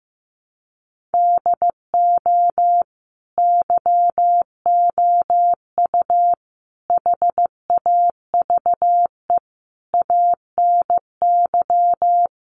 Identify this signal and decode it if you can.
Morse code, heard as a keyed tone, about 15 wpm, 700 Hz. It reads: DO YOU HAVE ANY
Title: Sentences from Top 600 Words 15wpm